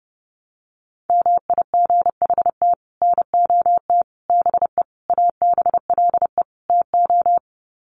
Morse code, heard as a keyed tone, about 30 wpm, 700 Hz. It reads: MIGHT NOT BE ABLE TO
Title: Sets of 5 Words Encore2 30wpm